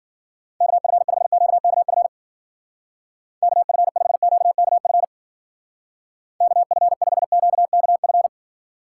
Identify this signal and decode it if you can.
Morse code, heard as a keyed tone, about 50 wpm, 700 Hz. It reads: KF5QKF KF5QKF KF5QKF
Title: Speed Racing - USA 2x3 Call Signs 40wpm